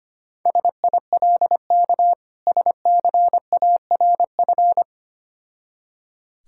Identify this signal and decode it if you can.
Morse code, heard as a keyed tone, about 25 wpm, 700 Hz. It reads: SILK SCARF
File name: practice / 25wpm / Difficult Words with Hints - 25wpm